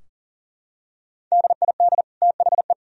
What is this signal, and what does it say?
Morse code, heard as a keyed tone, about 40 wpm, 700 Hz. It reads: DID THE
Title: Sentences from Top 1000 Words - Review Part 4 - 40wpm